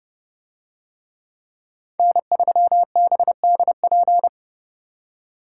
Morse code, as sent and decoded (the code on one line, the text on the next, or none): -. ...-- -... -.. .--.
N3BDP